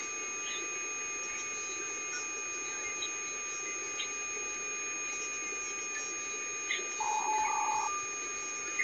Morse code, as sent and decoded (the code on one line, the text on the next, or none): none